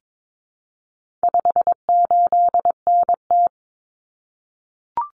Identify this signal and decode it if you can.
Morse code, heard as a keyed tone, about 22 wpm, 700 Hz. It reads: 58NT
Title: Common Cut Numbers - 4-Digit 22wpm